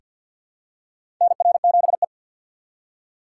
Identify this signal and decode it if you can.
Morse code, heard as a keyed tone, about 47 wpm, 700 Hz. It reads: NR7E
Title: USA Call Signs - 4x Character Spacing - 50wpm